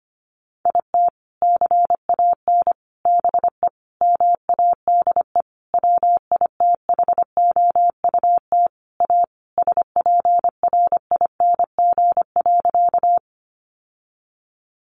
Morse code, heard as a keyed tone, about 25 wpm, 700 Hz. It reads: IT CAN BE MADE WITHOUT A SPRING.
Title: Sentences from Top 1200 Words 25wpm